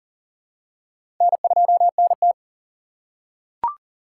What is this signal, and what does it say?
Morse code, heard as a keyed tone, about 40 wpm, 700 Hz. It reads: N2NT
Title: USA Call Signs - 8x Character Spacing - 40wpm